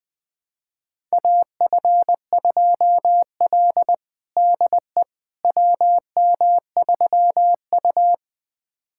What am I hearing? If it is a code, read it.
Morse code, heard as a keyed tone, about 20 wpm, 700 Hz. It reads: AF2L DE WM3U
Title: USA Callsign DE Callsign - 4 Characters - 20wpm